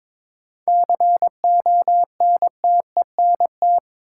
Morse code, heard as a keyed tone, about 22 wpm, 700 Hz. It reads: CONTENT
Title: Sentences from Top 900 Words - New Word Drill - 22wpm